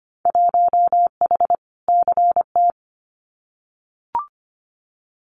Morse code, heard as a keyed tone, about 25 wpm, 700 Hz. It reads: 1H CT
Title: Winter Field Day Exchanges 25wpm